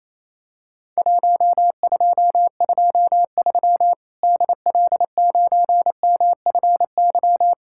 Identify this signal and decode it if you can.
Morse code, heard as a keyed tone, about 28 wpm, 700 Hz. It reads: 1223 DL9MFY
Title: Worked All Europe DX Contest - QTC Exchange - 28wpm